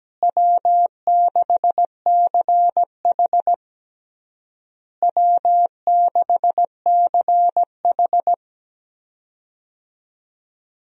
Morse code, heard as a keyed tone, about 17 wpm, 700 Hz. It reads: W6CH W6CH
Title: USA Call Signs - 4 Characters - Repeated 3x - 17wpm